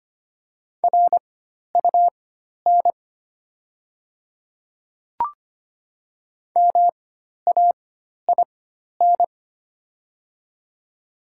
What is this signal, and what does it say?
Morse code, heard as a keyed tone, about 25 wpm, spaced as for 12 wpm, 700 Hz. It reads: RUN MAIN
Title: 4x Character Spacing - Top 300 Words - 25wpm